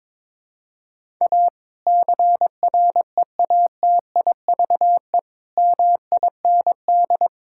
Morse code, heard as a keyed tone, about 21 wpm, 700 Hz. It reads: A CREATIVE MIND
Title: Sentences from Top 1000 Words - Review Part 5 - 22wpm